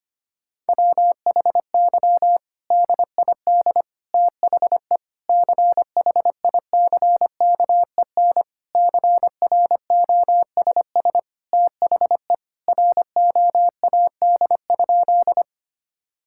Morse code, heard as a keyed tone, about 25 wpm, 700 Hz. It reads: WHY DID THE CHICKEN CROSS THE ROAD?
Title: Sentences from Top 1000 Words - Review Part 5 - 25wpm